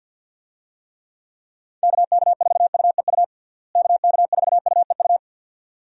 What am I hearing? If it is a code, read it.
Morse code, heard as a keyed tone, about 51 wpm, 700 Hz. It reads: KK4UEU KK4UEU
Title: USA Call Signs - 6 Characters - Repeated 2x - 50wpm